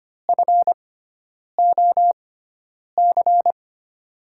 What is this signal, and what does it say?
Morse code, heard as a keyed tone, about 25 wpm, spaced as for 9 wpm, 700 Hz. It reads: FOC
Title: USA Call Signs - 5 Characters - 6x Character Spacing - 25wpm